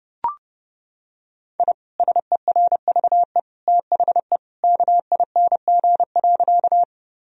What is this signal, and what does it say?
Morse code, heard as a keyed tone, about 30 wpm, 700 Hz. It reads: I SERVE THE KING.
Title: Sentences from Top 1000 Words - Review Part 2 - 30wpm